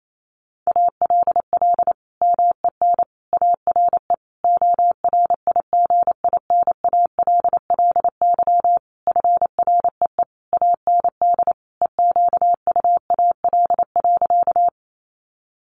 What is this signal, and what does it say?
Morse code, heard as a keyed tone, about 28 wpm, 700 Hz. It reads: ALL MEN ARE ORIGINALLY FREE AND EQUAL.
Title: Sentences from Top 1500 Words 28wpm